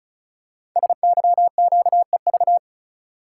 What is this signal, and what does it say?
Morse code, heard as a keyed tone, about 36 wpm, 690 Hz. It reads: SYQEV
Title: Code Groups - 5 Letters - 35wpm